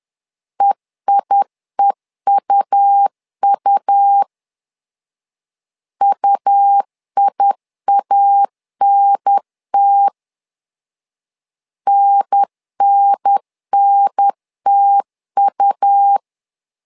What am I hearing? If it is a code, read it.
Morse code, heard as a keyed tone, about 10 wpm, 800 Hz. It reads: EIEUU UIANT NNNTU
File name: LEZIONI / 2 UHDN / 2 UHDN 10wpm 1904